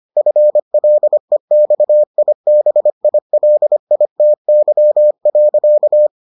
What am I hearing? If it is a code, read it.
Morse code, heard as a keyed tone, about 25 wpm, 600 Hz. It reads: FLEXIBILITY.